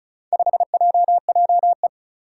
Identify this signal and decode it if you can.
Morse code, heard as a keyed tone, about 36 wpm, 700 Hz. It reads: 5JJE
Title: Four Challenging Letter-Number Combinations 35wpm